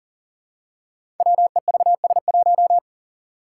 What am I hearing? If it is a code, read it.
Morse code, heard as a keyed tone, about 40 wpm, 700 Hz. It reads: WEVS1
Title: Five Challenging Letter-Number Combinations 40wpm